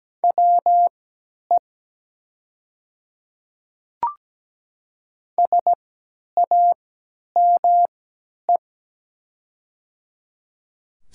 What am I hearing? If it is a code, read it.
Morse code, heard as a keyed tone, about 17 wpm, spaced as for 10 wpm, 700 Hz. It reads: WE SAME
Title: 3x Character Spacing - Top 300 Words - 17wpm